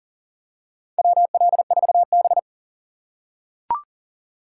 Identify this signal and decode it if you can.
Morse code, heard as a keyed tone, about 37 wpm, 700 Hz. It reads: WL4B
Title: USA Call Signs - 4 Characters - 6x Character Spacing - 40wpm